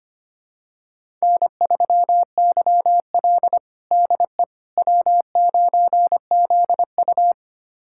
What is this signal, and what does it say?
Morse code, heard as a keyed tone, about 25 wpm, 700 Hz. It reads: N3YL DE W9ZU
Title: USA Callsign DE Callsign - 4 Characters - 25wpm